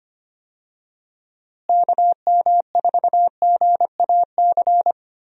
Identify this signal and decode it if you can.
Morse code, heard as a keyed tone, about 25 wpm, 700 Hz. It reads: KM4GAC
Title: USA Call Signs - 6 Characters - Repeated 3x - 25wpm